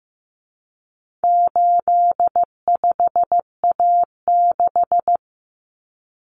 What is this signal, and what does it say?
Morse code, heard as a keyed tone, about 15 wpm, 700 Hz. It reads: 85A6
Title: Common Cut Numbers - 4-Digit 15wpm